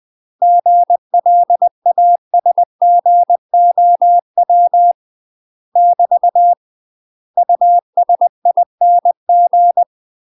Morse code, heard as a keyed tone, about 20 wpm, 700 Hz. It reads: GLASGOW <BT> USING
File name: UK MASTER QSOs TRAINING / 20WPM / UK MASTER QSOs 0072